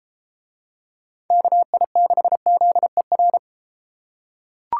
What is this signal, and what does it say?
Morse code, heard as a keyed tone, about 32 wpm, 700 Hz. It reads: KI6ZER